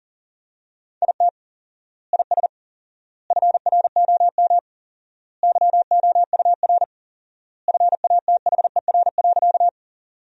Morse code, heard as a keyed tone, about 40 wpm, 700 Hz. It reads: IT IS FROM YOUR FATHER.
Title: 4x Word-Spacing - Sentences from Top 300 Words - 40wpm